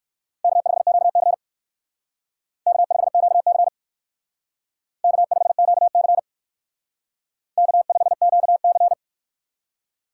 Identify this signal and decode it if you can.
Morse code, heard as a keyed tone, about 57 wpm, 700 Hz. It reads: K5QC K5QC K5QC K5QC